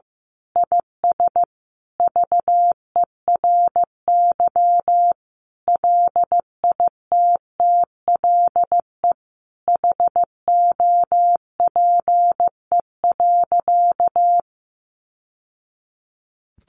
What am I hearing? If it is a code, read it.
Morse code, heard as a keyed tone, about 15 wpm, 700 Hz. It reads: IS VERY LITTLE HOPE.
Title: Sentences from Top 1000 Words 15wpm